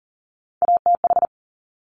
Morse code, heard as a keyed tone, about 39 wpm, 700 Hz. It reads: ATH
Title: Top Three Letter Combinations in Words 40wpm